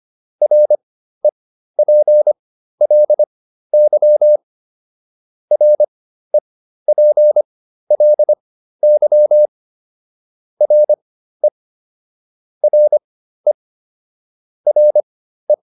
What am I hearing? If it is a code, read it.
Morse code, heard as a keyed tone, about 25 wpm, spaced as for 13 wpm, 600 Hz. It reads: REPLY REPLY RE RE RE